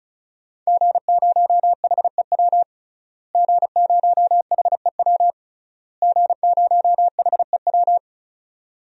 Morse code, heard as a keyed tone, about 34 wpm, 700 Hz. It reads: G0HEW G0HEW G0HEW
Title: DXCC-100 Call Signs - 5 Characters - Repeated 3x - 35wpm